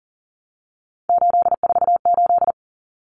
Morse code, heard as a keyed tone, about 40 wpm, 700 Hz. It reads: 848
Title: Common Cut Numbers - 3-Digit 40wpm